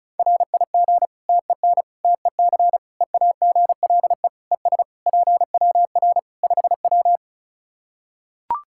Morse code, heard as a keyed tone, about 35 wpm, 700 Hz. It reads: RIG TEN TEC EAGLE ES PWR 5W
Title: Practice QSO - Exchange 2 - Info - 35wpm